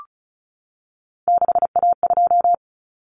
Morse code, heard as a keyed tone, about 34 wpm, 700 Hz. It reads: 6A2
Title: Common Cut Numbers - 3-Digit 35wpm